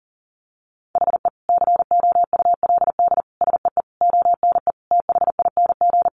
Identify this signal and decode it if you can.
Morse code, heard as a keyed tone, about 40 wpm, 700 Hz. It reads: HE COULD SEE ONE THING
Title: Sentences from Top 1100 Words 40wpm